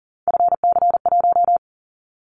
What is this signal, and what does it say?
Morse code, heard as a keyed tone, about 40 wpm, 700 Hz. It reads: FC1
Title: Callsign Prefixes for Top 25 DXCC - 40wpm